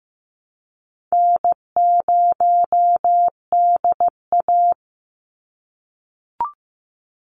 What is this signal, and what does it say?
Morse code, heard as a keyed tone, about 15 wpm, 700 Hz. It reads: N0DA
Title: NA SOTA Chaser Callsigns 15wpm